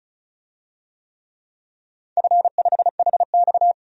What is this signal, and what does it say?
Morse code, heard as a keyed tone, about 36 wpm, 700 Hz. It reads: F5HX
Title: DXCC-100 Call Signs - 4 Characters - 35wpm